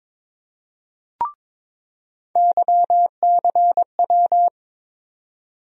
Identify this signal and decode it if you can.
Morse code, heard as a keyed tone, about 21 wpm, 700 Hz. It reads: YCW
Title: USA Call Sign Postfixes - 3 Letters 22wpm